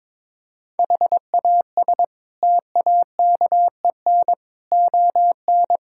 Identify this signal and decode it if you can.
Morse code, heard as a keyed tone, about 22 wpm, 700 Hz. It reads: HAS TAKEN ON
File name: practice / 22wpm / Sets of 8 Words - 22wpm